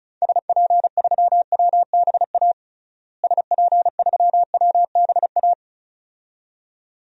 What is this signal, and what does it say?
Morse code, heard as a keyed tone, about 34 wpm, 700 Hz. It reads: SP3WBA SP3WBA
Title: DXCC-100 Call Signs - 6 Characters - Repeated 2x - 35wpm